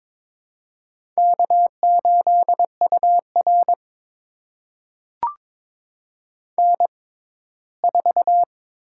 Morse code, heard as a keyed tone, about 22 wpm, spaced as for 15 wpm, 700 Hz. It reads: K8UR N4
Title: USA Call Signs - 4 Characters - 6x Character Spacing - 22wpm